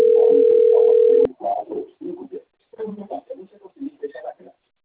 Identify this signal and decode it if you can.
Ringback tone (North America), 440 + 480 Hz, with speech.
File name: out- - - - - .72386